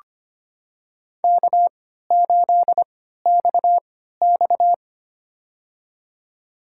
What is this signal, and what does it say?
Morse code, heard as a keyed tone, about 25 wpm, spaced as for 14 wpm, 700 Hz. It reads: K8XX
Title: USA Call Signs - 4 Characters - 3x Character Spacing - 25wpm